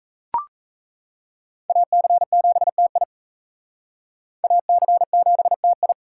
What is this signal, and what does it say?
Morse code, heard as a keyed tone, about 34 wpm, 700 Hz. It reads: AC7TI AC7TI
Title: Speed Racing - USA Call Signs 28wpm